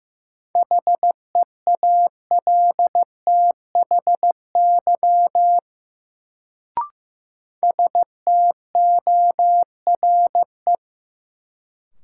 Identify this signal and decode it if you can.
Morse code, heard as a keyed tone, about 15 wpm, 700 Hz. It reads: HEALTHY STORE
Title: Sentences from Top 1000 Words - Review Part 5 - New Word Drill - 15wpm